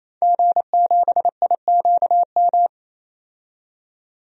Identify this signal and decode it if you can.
Morse code, heard as a keyed tone, about 28 wpm, 700 Hz. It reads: G7IQM
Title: DXCC-100 Call Signs - 5 Characters - 28wpm